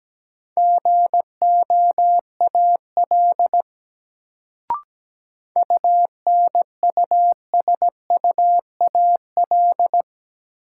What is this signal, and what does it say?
Morse code, heard as a keyed tone, about 17 wpm, 700 Hz. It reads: GOAL UNUSUAL
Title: Sentences from Top 1400 Words - New Word Drill Alt - 17wpm